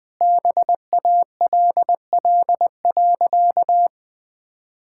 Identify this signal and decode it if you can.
Morse code, heard as a keyed tone, about 20 wpm, 700 Hz. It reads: BALL.
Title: Sentences from Top 1000 Words - Review Part 6 - 20wpm